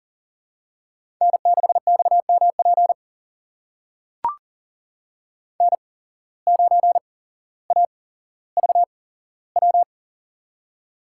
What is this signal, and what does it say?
Morse code, heard as a keyed tone, about 39 wpm, spaced as for 17 wpm, 700 Hz. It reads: N6XMP N9AVW
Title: USA Call Signs - 5 Characters - 8x Character Spacing - 40wpm